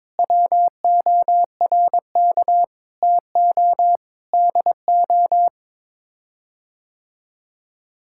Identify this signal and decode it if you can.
Morse code, heard as a keyed tone, about 22 wpm, 700 Hz. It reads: WORK TO DO
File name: Sets of 9 Words - 22wpm